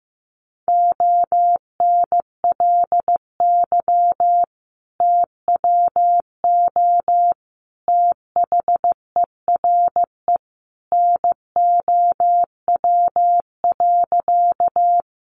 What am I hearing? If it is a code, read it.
Morse code, heard as a keyed tone, about 15 wpm, 700 Hz. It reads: ONLY TWO THERE NOW.
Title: Sentences from Top 200 Words 15wpm